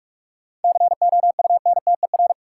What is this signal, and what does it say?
Morse code, heard as a keyed tone, about 46 wpm, 700 Hz. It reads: COUNTER
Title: Seven Letter Words - 45wpm